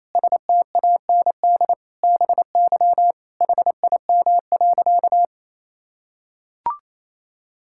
Morse code, heard as a keyed tone, about 28 wpm, 700 Hz. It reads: STAND BY HIM.
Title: Sentences from Top 1000 Words - Review Part 5 - 28wpm